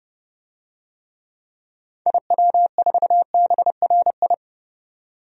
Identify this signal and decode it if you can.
Morse code, heard as a keyed tone, about 31 wpm, 700 Hz. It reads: IW4BRI
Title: DXCC-100 Call Signs - 6 Characters - 30wpm